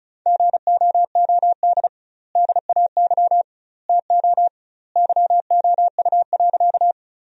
Morse code, heard as a keyed tone, about 36 wpm, 700 Hz. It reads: GOOD DAY TO YOU.
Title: 2x Word-Spacing - Sentences from Top 200 Words - 35wpm